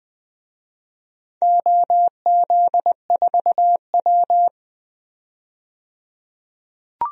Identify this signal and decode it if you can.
Morse code, heard as a keyed tone, about 20 wpm, 700 Hz. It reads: OZ4W
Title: DXCC-100 Call Signs - 4 Characters - 20wpm